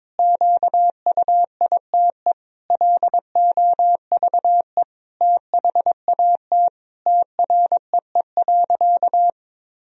Morse code, heard as a keyed tone, about 21 wpm, 700 Hz. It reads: QUITE LOVE THAT TREE.
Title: Sentences from Top 1000 Words - Review Part 1 - 22wpm